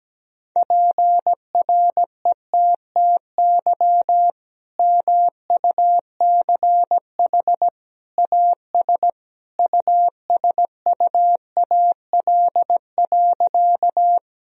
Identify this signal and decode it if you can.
Morse code, heard as a keyed tone, about 17 wpm, 700 Hz. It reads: PRETTY MUCH AS USUAL.
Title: Sentences from Top 1000 Words - Review Part 2 - 17wpm